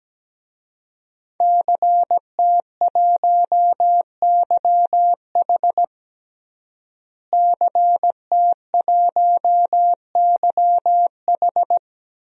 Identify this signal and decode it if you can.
Morse code, heard as a keyed tone, about 17 wpm, 700 Hz. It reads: CT1YH CT1YH
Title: DXCC-100 Call Signs - 5 Characters - Repeated 2x - 17wpm